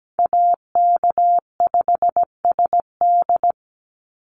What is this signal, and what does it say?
Morse code, heard as a keyed tone, about 17 wpm, 700 Hz. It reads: AK5SD
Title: NA SOTA Chaser Callsigns 17wpm